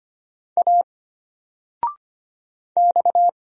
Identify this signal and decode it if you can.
Morse code, heard as a keyed tone, about 25 wpm, 700 Hz. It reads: A X
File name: Single-Letter-Number-25wpm